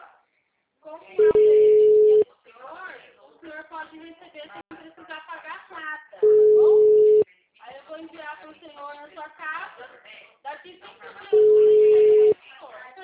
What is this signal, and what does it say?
Ringback tone (Europe, 425 Hz), with speech.